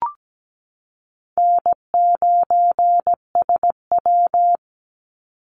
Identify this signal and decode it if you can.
Morse code, heard as a keyed tone, about 17 wpm, 700 Hz. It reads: N9SW
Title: Easy Call Signs Repeated 3x - 17wpm